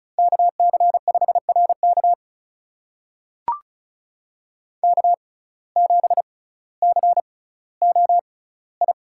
Morse code, heard as a keyed tone, about 36 wpm, spaced as for 18 wpm, 700 Hz. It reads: KC5RK K7COI
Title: USA Call Signs - 5 Characters - 6x Character Spacing - 35wpm